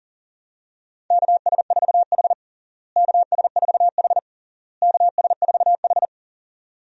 Morse code, heard as a keyed tone, about 40 wpm, 700 Hz. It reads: KS4H KS4H KS4H
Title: USA Call Signs - 4 Characters - Repeated 3x - 40wpm